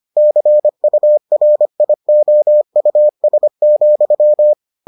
Morse code, heard as a keyed tone, about 25 wpm, 600 Hz. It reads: CURIOUS,